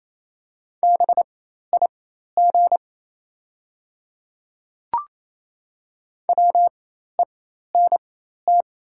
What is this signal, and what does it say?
Morse code, heard as a keyed tone, about 28 wpm, spaced as for 13 wpm, 690 Hz. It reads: BIG WENT
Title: 4x Character Spacing - Top 200 Words - 28wpm